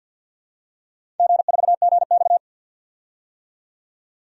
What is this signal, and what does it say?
Morse code, heard as a keyed tone, about 52 wpm, 700 Hz. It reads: G4GX
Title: DXCC-100 Call Signs - 4 Characters - 50wpm